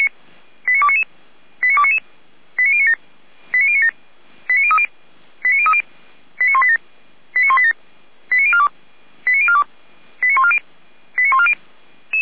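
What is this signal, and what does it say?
Five-tone selective call (ZVEI), heard as quick runs of five tones, tones of about 70 ms. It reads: 89299 89299 89097 89097 89030 89030 89187 89187 89042 89042 89140 89140
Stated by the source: ZVEI five-tone call received by SDR.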